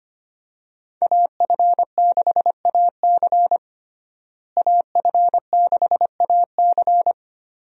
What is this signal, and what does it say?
Morse code, heard as a keyed tone, about 25 wpm, 700 Hz. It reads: AF6AC AF6AC
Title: USA Call Signs - 5 Characters - Repeated 3x - 25wpm